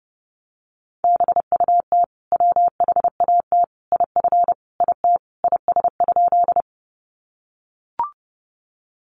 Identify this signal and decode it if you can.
Morse code, heard as a keyed tone, about 30 wpm, 700 Hz. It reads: BUT WHAT IF IT IS?
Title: Sentences from Top 100 Words 30wpm